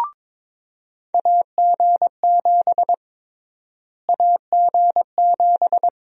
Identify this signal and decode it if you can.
Morse code, heard as a keyed tone, about 22 wpm, 700 Hz. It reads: AG7 AG7
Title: USA Call Sign Prefixes - Repeated 2x - 22wpm